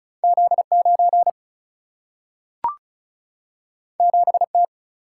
Morse code, heard as a keyed tone, about 36 wpm, 700 Hz. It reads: Z9 7T
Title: Single-Letter Single-Number 35wpm